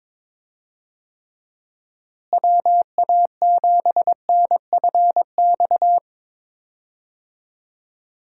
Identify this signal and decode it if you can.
Morse code, heard as a keyed tone, about 22 wpm, 700 Hz. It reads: WA7NFX
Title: USA Call Signs - 6 Characters - Repeated 2x - 22wpm